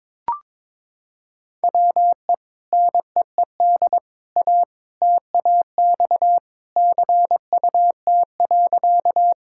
Morse code, heard as a keyed tone, about 22 wpm, 700 Hz. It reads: WE NEED A TAX CUT.
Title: Sentences from Top 1000 Words - Review Part 5 - 22wpm